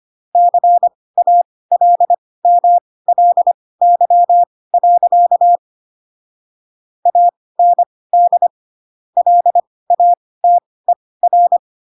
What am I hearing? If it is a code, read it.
Morse code, heard as a keyed tone, about 25 wpm, spaced as for 18 wpm, 700 Hz. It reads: CALMLY. AND LATER
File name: Pinocchio / Pinocchio ch7 12 25 ik1hge